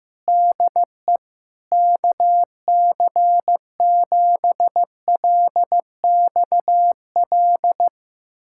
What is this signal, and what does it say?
Morse code, heard as a keyed tone, about 16 wpm, 700 Hz. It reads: DE KC7LXL
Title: USA Callsign DE Callsign - 6 Characters - 15wpm